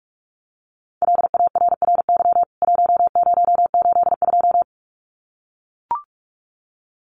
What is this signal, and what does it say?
Morse code, heard as a keyed tone, about 44 wpm, 700 Hz. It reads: LARRY 1082
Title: CWT Exchange 45wpm